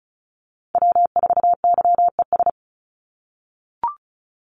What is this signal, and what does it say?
Morse code, heard as a keyed tone, about 34 wpm, 700 Hz. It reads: W4YES
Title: Call Signs - 35wpm